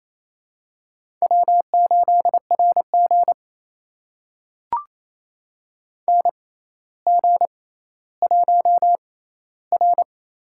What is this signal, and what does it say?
Morse code, heard as a keyed tone, about 28 wpm, spaced as for 14 wpm, 700 Hz. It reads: W8RG NG1R
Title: USA Call Signs - 4 Characters - 6x Character Spacing - 28wpm